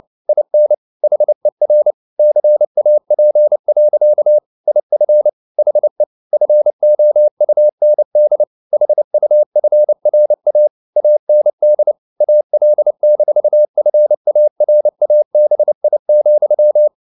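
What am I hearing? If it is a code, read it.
Morse code, heard as a keyed tone, about 29 wpm, 600 Hz. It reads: IN HER CAP. IF HE FOUND HUFRA AND AL-FARABI,
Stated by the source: Morse code at 29 wpm, sending Ken Follett's novel Never in English, from CW med Gnister.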